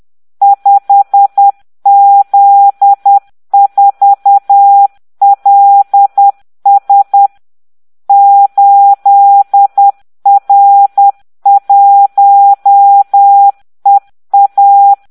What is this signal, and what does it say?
Morse code, heard as a keyed tone, about 10 wpm, 800 Hz. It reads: 5Z4LS 8R1EA